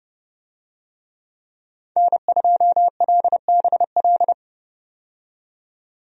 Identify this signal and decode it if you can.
Morse code, heard as a keyed tone, about 32 wpm, 700 Hz. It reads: N2LBL
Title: USA Call Signs - 5 Characters - 30wpm